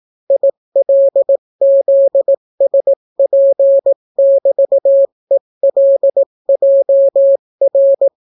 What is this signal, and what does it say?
Morse code, heard as a keyed tone, about 18 wpm, 550 Hz. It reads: ILZSP<BT>ELJR